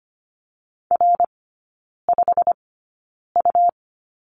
Morse code, heard as a keyed tone, about 25 wpm, spaced as for 9 wpm, 700 Hz. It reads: R5U
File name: Lesson 17e - Letter u - Three Characters 25wpm